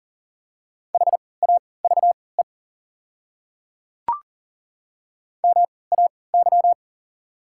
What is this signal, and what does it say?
Morse code, heard as a keyed tone, about 40 wpm, spaced as for 23 wpm, 690 Hz. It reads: HAVE MAY
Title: 3x Character Spacing - Top 200 Words - 40wpm